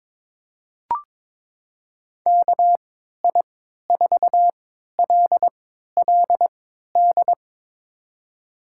Morse code, heard as a keyed tone, about 22 wpm, spaced as for 13 wpm, 700 Hz. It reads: KI4LLD
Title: USA Call Signs - 6 Characters - 3x Character Spacing - 22wpm